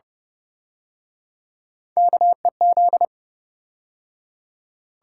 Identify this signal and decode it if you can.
Morse code, heard as a keyed tone, about 31 wpm, 700 Hz. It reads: KEZ